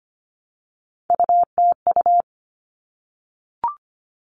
Morse code, heard as a keyed tone, about 25 wpm, 700 Hz. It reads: UTU
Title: All Cut Numbers - 3-Digit 25wpm